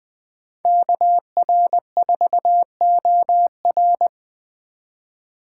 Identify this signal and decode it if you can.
Morse code, heard as a keyed tone, about 20 wpm, 700 Hz. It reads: KR4OR